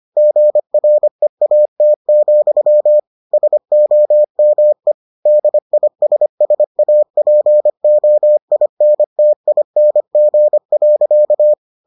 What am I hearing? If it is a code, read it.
Morse code, heard as a keyed tone, about 25 wpm, 600 Hz. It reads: GREAT, SOME DISSAPOINTING.